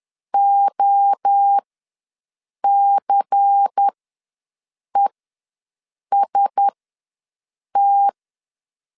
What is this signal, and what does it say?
Morse code, heard as a keyed tone, about 10 wpm, spaced as for 6 wpm, 800 Hz. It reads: OCEST